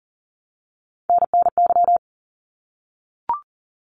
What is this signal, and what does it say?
Morse code, heard as a keyed tone, about 40 wpm, 700 Hz. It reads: NNY
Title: ARRL Sections 40wpm